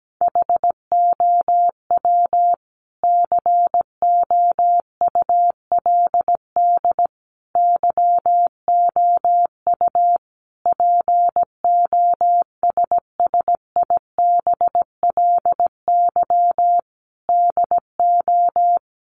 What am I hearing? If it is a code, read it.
Morse code, heard as a keyed tone, about 17 wpm, 700 Hz. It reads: HOW COULD YOU POSSIBLY DO
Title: Sentences from Top 1300 Words 17wpm